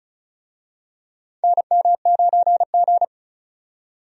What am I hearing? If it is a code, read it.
Morse code, heard as a keyed tone, about 34 wpm, 700 Hz. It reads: NM9G
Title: USA Call Signs - 4 Characters - 35wpm